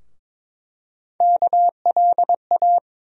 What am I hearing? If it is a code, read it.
Morse code, heard as a keyed tone, about 22 wpm, 700 Hz. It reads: KLA
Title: USA Call Sign Postfixes - 3 Letters 22wpm